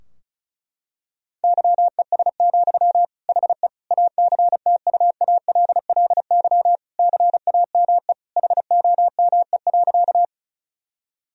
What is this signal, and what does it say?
Morse code, heard as a keyed tone, about 36 wpm, 700 Hz. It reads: YES, HE ACTUALLY CAME HOME.
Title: Sentences from Top 1000 Words - Review Part 4 - 35wpm